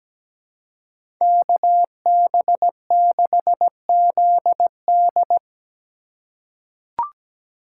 Morse code, heard as a keyed tone, about 17 wpm, 700 Hz. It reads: KB6ZD
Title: USA Call Signs - 5 Characters - 3x Character Spacing - 17wpm